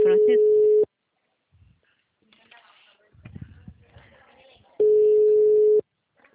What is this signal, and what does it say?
Ringback tone (Europe, 425 Hz), with speech.
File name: out- -2025- - - .4331